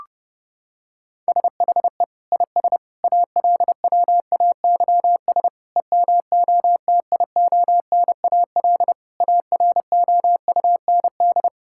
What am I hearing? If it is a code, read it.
Morse code, heard as a keyed tone, about 30 wpm, 700 Hz. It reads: SHE IS ALWAYS EMOTIONAL AROUND
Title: Sentences from Top 1000 Words - Review Part 5 - 30wpm